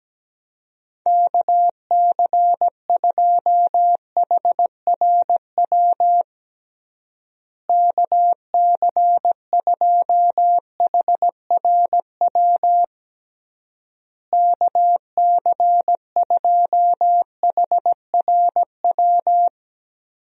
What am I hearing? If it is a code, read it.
Morse code, heard as a keyed tone, about 17 wpm, 700 Hz. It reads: KC2HRW KC2HRW KC2HRW
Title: USA Call Signs - 6 Characters - Repeated 3x - 17wpm